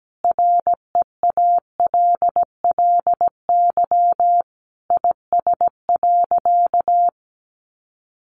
Morse code, heard as a keyed tone, about 17 wpm, 700 Hz. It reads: REALLY IS.
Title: Sentences from Top 600 Words 17wpm